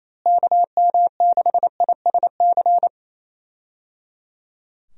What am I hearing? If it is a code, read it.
Morse code, heard as a keyed tone, about 27 wpm, 700 Hz. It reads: KM6ISC
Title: USA Call Signs - 6 Characters - Repeated 3x - 28wpm